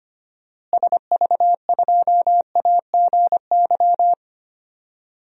Morse code, heard as a keyed tone, about 25 wpm, 700 Hz. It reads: SV2AGY
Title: DXCC-100 Call Signs - 6 Characters - Repeated 2x - 25wpm